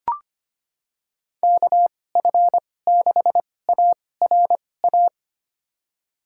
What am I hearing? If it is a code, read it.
Morse code, heard as a keyed tone, about 25 wpm, spaced as for 18 wpm, 700 Hz. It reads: KF6ARA
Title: USA Call Signs - 6 Characters - 2x Character Spacing - 25wpm